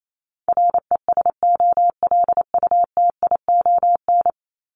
Morse code, heard as a keyed tone, about 28 wpm, 700 Hz. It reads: RESOLUTION
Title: Top 1501-1750 Words 28wpm